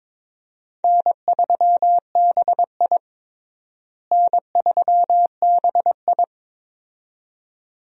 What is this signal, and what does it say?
Morse code, heard as a keyed tone, about 22 wpm, 700 Hz. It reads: N3BI N3BI
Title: USA Call Signs - 4 Characters - Repeated 2x - 22wpm